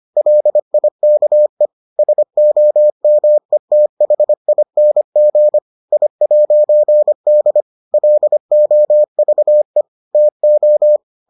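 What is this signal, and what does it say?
Morse code, heard as a keyed tone, about 25 wpm, 600 Hz. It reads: LIKE SOMETHING I'D LOVE TO
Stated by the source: Morse edition of the Foundations of Amateur Radio episodes.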